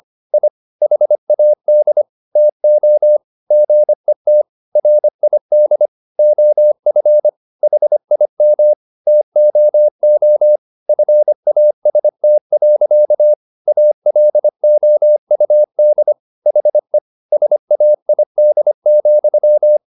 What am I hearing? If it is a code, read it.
Morse code, heard as a keyed tone, about 25 wpm, 600 Hz. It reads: I HAD TO GET RID OF HIM TOO FAST. ALOUD HE SAID,
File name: the old man and the sea part 1 25wpm